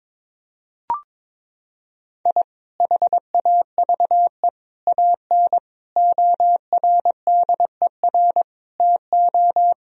Morse code, heard as a keyed tone, about 21 wpm, 700 Hz. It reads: I HAVE AN ORDER TO
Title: Sentences from Top 1000 Words - Review Part 2 - 22wpm